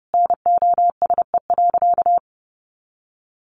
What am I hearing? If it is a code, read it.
Morse code, heard as a keyed tone, about 30 wpm, 700 Hz. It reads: NOSE.
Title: Easy Phrases 30wpm